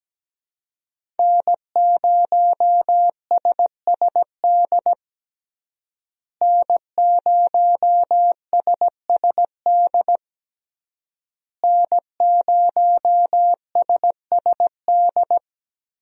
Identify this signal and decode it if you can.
Morse code, heard as a keyed tone, about 17 wpm, 700 Hz. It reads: N0SSD N0SSD N0SSD
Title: USA Call Signs - 5 Characters - Repeated 3x - 17wpm